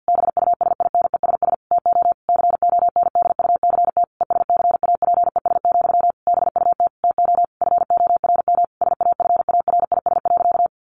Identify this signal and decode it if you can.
Morse code, heard as a keyed tone, about 50 wpm, 700 Hz. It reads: BUSINESS TO CONDUCT ESCAPES, BUT TO FORM SAFARIS.
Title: A Sound Of Thunder 50wpm